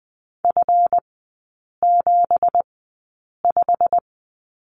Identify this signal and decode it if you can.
Morse code, heard as a keyed tone, about 20 wpm, spaced as for 8 wpm, 700 Hz. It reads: F75
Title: 166 Lesson 27e - Number 7 - Three Characters 20wpm